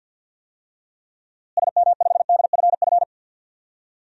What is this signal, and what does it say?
Morse code, heard as a keyed tone, about 53 wpm, 700 Hz. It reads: SM5DFF
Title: DXCC-100 Call Signs - 6 Characters - Repeated 2x - 50wpm